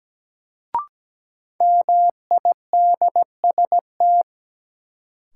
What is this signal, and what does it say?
Morse code, heard as a keyed tone, about 17 wpm, 700 Hz. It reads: MIDST
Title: Sets of 4 Words - New Word Drill - 17wpm